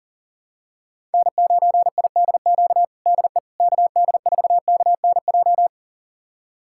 Morse code, heard as a keyed tone, about 40 wpm, 700 Hz. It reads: N9IDQ DE KD4KNJ
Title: USA Callsign DE Callsign 40wpm